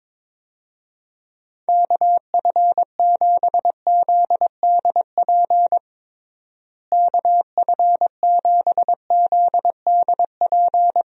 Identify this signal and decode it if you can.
Morse code, heard as a keyed tone, about 22 wpm, 700 Hz. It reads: KF7ZDP KF7ZDP
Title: USA Call Signs - 6 Characters - Repeated 3x - 22wpm